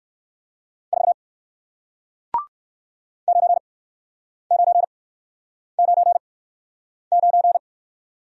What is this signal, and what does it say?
Morse code, heard as a keyed tone, about 59 wpm, spaced as for 9 wpm, 700 Hz. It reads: 4 9999